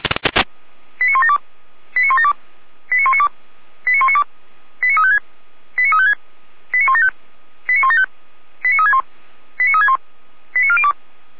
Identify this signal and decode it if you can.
Five-tone selective call (ZVEI), heard as quick runs of five tones, tones of about 70 ms. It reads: 89172 89172 89192 89192 89356 89356 89175 89175 89361 89361 89402
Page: SDR/ZVEI